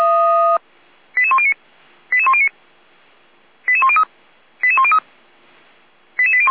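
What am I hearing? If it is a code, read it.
Five-tone selective call (ZVEI), heard as quick runs of five tones, tones of about 70 ms. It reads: 88109 88109 88103 88103 88801
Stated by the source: Five-tone ZVEI selective call 88102 received by SDR, from audio.f85360.feuerwehr-orlamuende.org fire brigade server.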